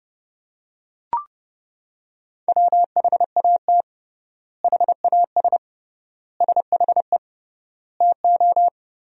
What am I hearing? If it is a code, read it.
Morse code, heard as a keyed tone, about 30 wpm, 700 Hz. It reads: WHAT HAS SHE TO
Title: 3x Word-Spacing - Sentences from Top 100 Words - 30wpm